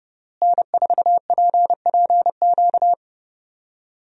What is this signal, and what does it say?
Morse code, heard as a keyed tone, about 32 wpm, 700 Hz. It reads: N4PPQ
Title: USA Call Signs - 5 Characters - Repeated 3x - 30wpm